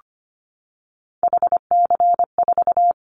Morse code, heard as a keyed tone, about 25 wpm, 700 Hz. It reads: HC4
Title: Callsign Prefixes for Top 100 DXCC - 25wpm